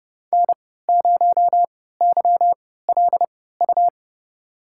Morse code, heard as a keyed tone, about 32 wpm, spaced as for 17 wpm, 700 Hz. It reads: N0YLU
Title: USA Call Signs - 5 Characters - 3x Character Spacing - 30wpm